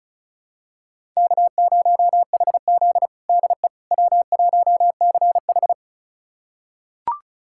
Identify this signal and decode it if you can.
Morse code, heard as a keyed tone, about 36 wpm, 700 Hz. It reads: K0HZ DE W1CH
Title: USA Callsign DE Callsign - 4 Characters - 35wpm